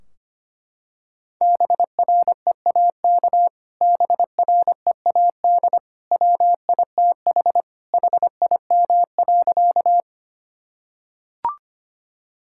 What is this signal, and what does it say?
Morse code, heard as a keyed tone, about 25 wpm, 700 Hz. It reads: BREAK BREAD WITH HIM.
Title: Sentences from Top 1600 Words 25wpm